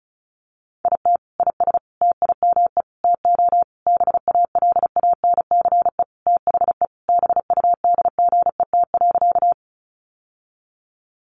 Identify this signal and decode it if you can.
Morse code, heard as a keyed tone, about 34 wpm, 700 Hz. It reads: IT IS TIME TO BALANCE THE BUDGET.
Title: Sentences from Top 1200 Words 35wpm